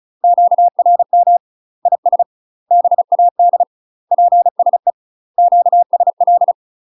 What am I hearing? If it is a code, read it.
Morse code, heard as a keyed tone, about 34 wpm, 700 Hz. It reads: QRM IS BAD PSE QSL
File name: UK MASTER QSOs TRAINING / 35WPM / UK MASTER QSOs 0042